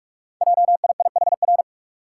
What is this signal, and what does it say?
Morse code, heard as a keyed tone, about 44 wpm, 690 Hz. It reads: JIIHR
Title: Code Groups - 5 Letters - 45wpm